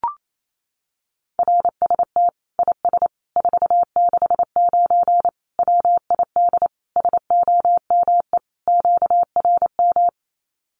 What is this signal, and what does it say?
Morse code, heard as a keyed tone, about 28 wpm, 700 Hz. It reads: RST IS 469 WID SOME QRM
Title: Common QSO Phrases 28wpm